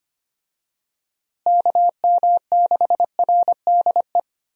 Morse code, heard as a keyed tone, about 25 wpm, 700 Hz. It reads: KM6RDE
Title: USA Call Signs - 6 Characters - Repeated 2x - 25wpm